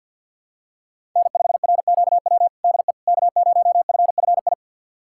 Morse code, heard as a keyed tone, about 46 wpm, 700 Hz. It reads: N5RQW DE K0FFI
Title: USA Callsign DE Callsign 50wpm